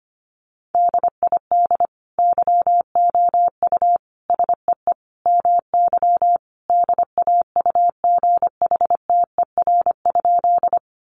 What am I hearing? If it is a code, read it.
Morse code, heard as a keyed tone, about 25 wpm, 700 Hz. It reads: DID YOU SEE MY DAUGHTER?
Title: Sentences from Top 1500 Words 25wpm